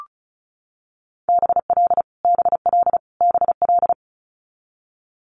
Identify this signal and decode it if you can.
Morse code, heard as a keyed tone, about 36 wpm, 700 Hz. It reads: BL BL BL
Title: Multi-letter Phonemes Repeated 3x 35wpm